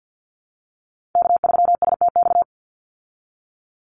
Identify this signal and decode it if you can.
Morse code, heard as a keyed tone, about 50 wpm, 700 Hz. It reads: K3STX
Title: Call Signs Repeated 3x - 50wpm